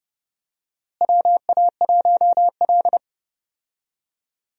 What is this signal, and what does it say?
Morse code, heard as a keyed tone, about 31 wpm, 700 Hz. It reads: WA1L
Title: USA Call Signs - 4 Characters - Repeated 2x - 30wpm